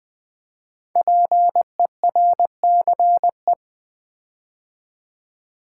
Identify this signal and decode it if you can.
Morse code, heard as a keyed tone, about 20 wpm, 700 Hz. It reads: PERCE
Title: Build A Word - 5 Letters At A Time - 10-Letter Words - 20wpm